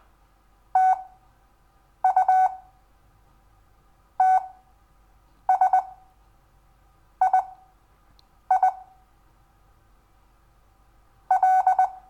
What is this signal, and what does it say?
Morse code, heard as a keyed tone, about 20 wpm, spaced as for 6 wpm, 750 Hz. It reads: TU TSII L
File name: cw / Lesson6 WI